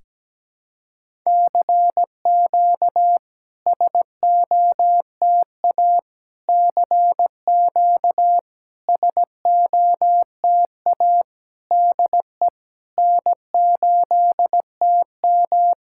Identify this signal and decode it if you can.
Morse code, heard as a keyed tone, about 17 wpm, 700 Hz. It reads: CQ SOTA CQ SOTA DE N8TM
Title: SOTA QSO Practice 17wpm